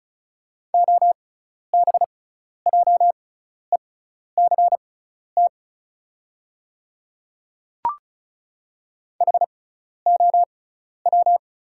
Morse code, heard as a keyed tone, about 34 wpm, spaced as for 12 wpm, 690 Hz. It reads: OBJECT HOW
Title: 6x Character Spacing - Top 500 Words - 35wpm